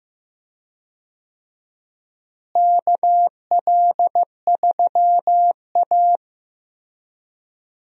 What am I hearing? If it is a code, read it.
Morse code, heard as a keyed tone, about 15 wpm, 700 Hz. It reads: KL3A
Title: USA Call Signs - 4 Characters - Repeated 2x - 15wpm